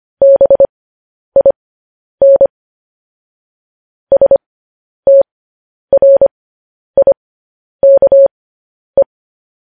Morse code, heard as a keyed tone, about 25 wpm, spaced as for 10 wpm, 550 Hz. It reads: BIN STRIKE
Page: news headlines in morse code at 10 wpm